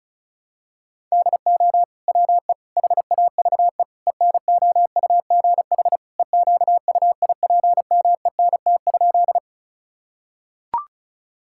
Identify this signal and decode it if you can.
Morse code, heard as a keyed tone, about 36 wpm, 700 Hz. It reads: DO WE HAVE ENOUGH EQUIPMENT?
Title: Sentences from Top 1000 Words - Review Part 5 - 35wpm